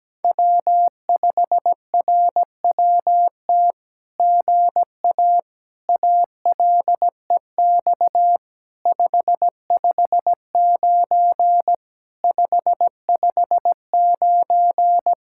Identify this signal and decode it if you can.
Morse code, heard as a keyed tone, about 17 wpm, 700 Hz. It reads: W5RWT GA ALEX 559 559
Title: SOTA QSO Practice 17wpm